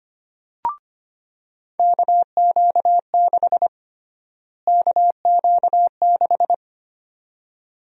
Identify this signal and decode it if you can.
Morse code, heard as a keyed tone, about 25 wpm, 700 Hz. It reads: KQ6 KQ6
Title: USA Call Sign Prefixes - Repeated 2x - 25wpm